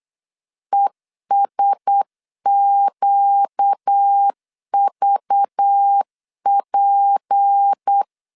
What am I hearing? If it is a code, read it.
Morse code, heard as a keyed tone, about 8 wpm, 800 Hz. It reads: ESQVP
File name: LEZIONI / 9 2 8wpm 3437